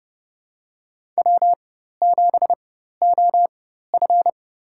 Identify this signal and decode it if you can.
Morse code, heard as a keyed tone, about 32 wpm, spaced as for 14 wpm, 700 Hz. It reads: W7OF
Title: USA Call Signs - 4 Characters - 4x Character Spacing - 30wpm